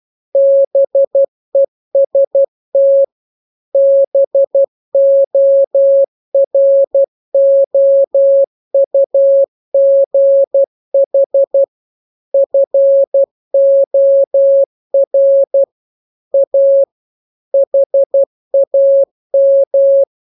Morse code, heard as a keyed tone, about 12 wpm, 550 Hz. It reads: BEST BOROUGH FOR A HAM
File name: jokes / joke 15 12 0000